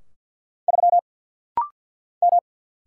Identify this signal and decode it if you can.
Morse code, heard as a keyed tone, about 50 wpm, 700 Hz. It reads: 3 M
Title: Single Letter-Number Alt - Rapid-Fire 50wpm